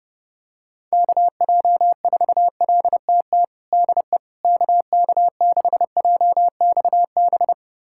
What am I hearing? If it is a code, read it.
Morse code, heard as a keyed tone, about 32 wpm, 700 Hz. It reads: KJ4LTT DE KK6JXB
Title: USA Callsign DE Callsign 30wpm